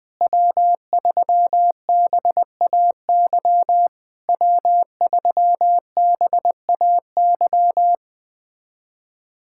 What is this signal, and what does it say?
Morse code, heard as a keyed tone, about 20 wpm, 700 Hz. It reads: W3BAY W3BAY
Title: POTA QSO Practice - 20wpm